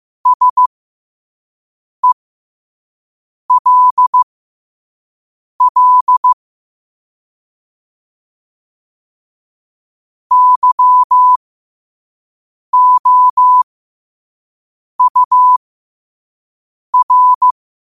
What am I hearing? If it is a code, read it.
Morse code, heard as a keyed tone, about 15 wpm, spaced as for 5 wpm, 1000 Hz. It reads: SELL YOUR
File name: qotdfiles / 5wpm